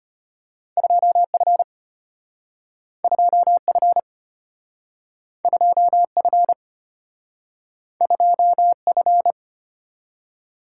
Morse code, heard as a keyed tone, about 34 wpm, 700 Hz. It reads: 2F 2F 2F 2F